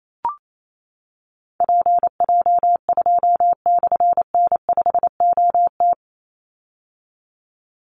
Morse code, heard as a keyed tone, about 28 wpm, 700 Hz. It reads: PJ2/N5OT
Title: Call Signs - 28wpm